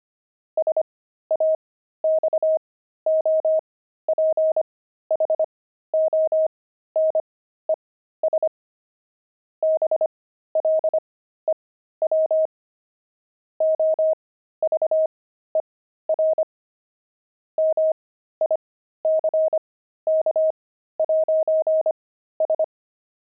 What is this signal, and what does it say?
Morse code, heard as a keyed tone, about 25 wpm, spaced as for 13 wpm, 630 Hz. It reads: SAXOPHONES BLEW OVER MICK'S
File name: pangram-fs-13-cs-25-saxophones